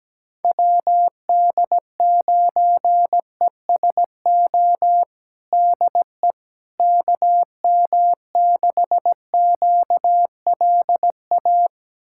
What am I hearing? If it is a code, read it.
Morse code, heard as a keyed tone, about 17 wpm, 700 Hz. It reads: WD9ESO DE KM6QLA
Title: USA Callsign DE Callsign 17wpm